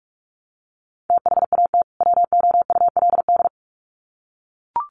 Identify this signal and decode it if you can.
Morse code, heard as a keyed tone, about 44 wpm, 700 Hz. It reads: THAT WOULD
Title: Sets of 2 Words 45wpm